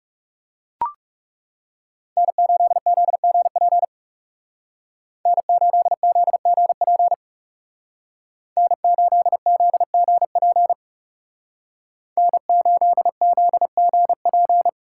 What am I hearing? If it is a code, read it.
Morse code, heard as a keyed tone, about 39 wpm, 700 Hz. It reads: N8ZGP N8ZGP N8ZGP N8ZGP